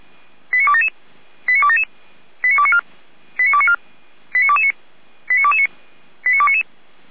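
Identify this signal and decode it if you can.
Five-tone selective call (ZVEI), heard as quick runs of five tones, tones of about 70 ms. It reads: 89277 89277 89294 89294 89229 89229 89299
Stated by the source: ZVEI five-tone call received on an SDR.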